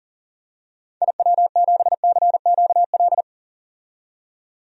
Morse code, heard as a keyed tone, about 40 wpm, 700 Hz. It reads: IW7CQL
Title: DXCC-100 Call Signs - 6 Characters - 40wpm